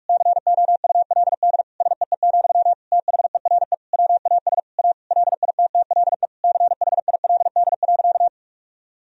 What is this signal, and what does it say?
Morse code, heard as a keyed tone, about 46 wpm, 700 Hz. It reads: COULD SEE, THERE WAS A LITTLE CHILD.
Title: Sentences from Top 1000 Words - Review Part 1 - 45wpm